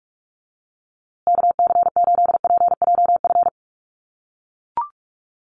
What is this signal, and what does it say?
Morse code, heard as a keyed tone, about 44 wpm, 700 Hz. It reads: KC8PJF
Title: USA Call Signs Repeated 3x 45wpm